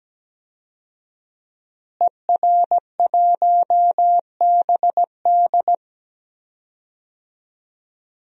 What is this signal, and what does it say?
Morse code, heard as a keyed tone, about 17 wpm, 700 Hz. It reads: ER1BD
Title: DXCC-100 Call Signs - 5 Characters - Repeated 3x - 17wpm